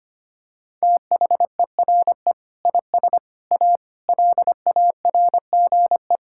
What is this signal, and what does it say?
Morse code, heard as a keyed tone, about 25 wpm, 700 Hz. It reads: THERE IS A LARGE
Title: Sentences from Top 1000 Words - Review Part 5 - 25wpm